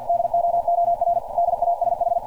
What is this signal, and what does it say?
Morse code, heard as a keyed tone, about 50 wpm, 690 Hz. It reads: DOOR DASH
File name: archived / CRSnet